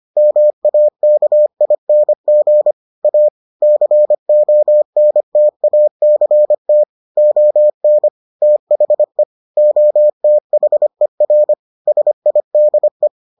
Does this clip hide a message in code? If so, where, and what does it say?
Yes: Morse code from 0.2 to 13.1 s, heard as a keyed tone, about 25 wpm, 600 Hz. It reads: MAKING A CONTACT ON THE OTHER SIDE